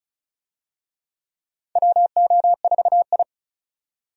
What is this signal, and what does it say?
Morse code, heard as a keyed tone, about 36 wpm, 700 Hz. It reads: WO4I
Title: USA Call Signs - 4 Characters - 35wpm